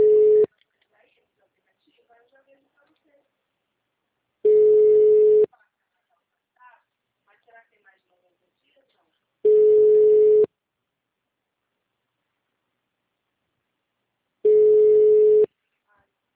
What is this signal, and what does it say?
Ringback tone (Europe, 425 Hz), with speech.